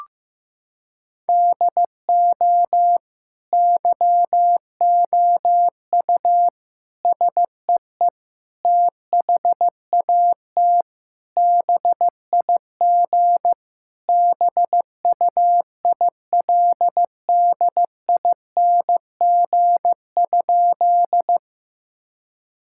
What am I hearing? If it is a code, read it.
Morse code, heard as a keyed tone, about 16 wpm, 700 Hz. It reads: DO YOU SEE THAT BIG BUILDING?
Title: Sentences from Top 1000 Words - Review Part 5 - 15wpm